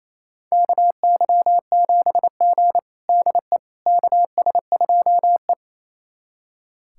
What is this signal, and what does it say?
Morse code, heard as a keyed tone, about 28 wpm, 700 Hz. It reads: KY7G DE KS2E
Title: USA Callsign DE Callsign - 4 Characters - 28wpm